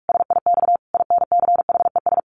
Morse code, heard as a keyed tone, about 44 wpm, 700 Hz. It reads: SIX INCHES
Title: A Sound Of Thunder 45wpm